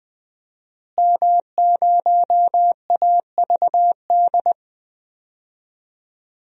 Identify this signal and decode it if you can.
Morse code, heard as a keyed tone, about 20 wpm, 700 Hz. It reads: M0AVD
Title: DXCC-100 Call Signs - 5 Characters - Repeated 2x - 20wpm